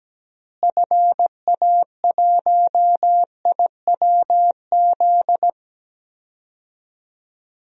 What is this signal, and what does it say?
Morse code, heard as a keyed tone, about 17 wpm, 700 Hz. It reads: FA1IWZ
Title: DXCC-100 Call Signs - 6 Characters - Repeated 3x - 17wpm